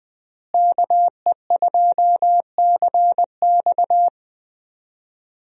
Morse code, heard as a keyed tone, about 20 wpm, 700 Hz. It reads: KE2CX
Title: USA Call Signs - 5 Characters - Repeated 2x - 20wpm